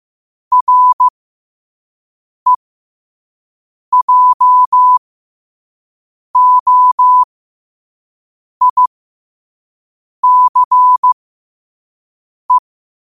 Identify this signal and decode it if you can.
Morse code, heard as a keyed tone, about 15 wpm, spaced as for 5 wpm, 1000 Hz. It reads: REJOICE
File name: qotdfiles / 5wpm